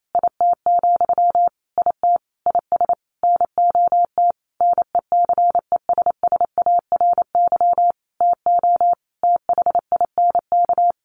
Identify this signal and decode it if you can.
Morse code, heard as a keyed tone, about 28 wpm, 700 Hz. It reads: IT, IT IS NOT NECESSARY TO THINK
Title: Sentences from Top 1300 Words 28wpm